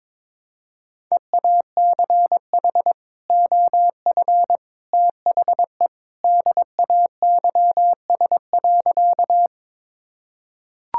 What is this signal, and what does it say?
Morse code, heard as a keyed tone, about 22 wpm, 700 Hz. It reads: EACH OF THE DAYS.
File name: practice / 22wpm / Sentences with 3-Word Phrases - 22wpm